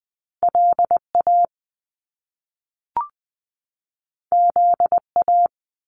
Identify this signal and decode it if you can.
Morse code, heard as a keyed tone, about 20 wpm, 700 Hz. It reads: L A Z A
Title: Top Two Letter Combinations in Words 20wpm